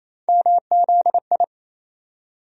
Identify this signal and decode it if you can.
Morse code, heard as a keyed tone, about 27 wpm, 700 Hz. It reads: MZI
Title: USA Call Sign Postfixes - 3 Letters 28wpm